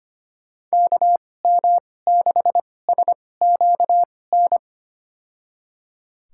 Morse code, heard as a keyed tone, about 25 wpm, spaced as for 18 wpm, 700 Hz. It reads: KM6SQN
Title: USA Call Signs - 6 Characters - 2x Character Spacing - 25wpm